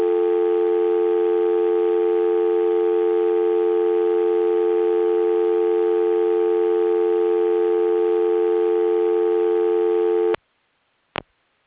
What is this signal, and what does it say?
Dial tone (North America), 350 + 440 Hz.